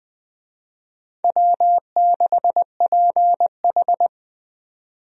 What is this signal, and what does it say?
Morse code, heard as a keyed tone, about 20 wpm, 700 Hz. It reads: W6PH
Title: USA Call Signs - 4 Characters - 20wpm